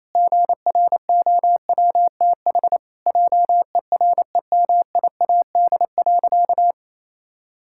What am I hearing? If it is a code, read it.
Morse code, heard as a keyed tone, about 28 wpm, 700 Hz. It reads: GROWTH JEREMIAD.
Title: Pangram Challenge v2 - 28wpm